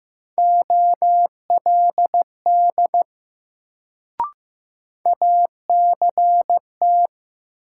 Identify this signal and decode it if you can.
Morse code, heard as a keyed tone, about 15 wpm, 700 Hz. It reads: OLD ACT
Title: Sentences from Top 200 Words - New Word Drill - 15wpm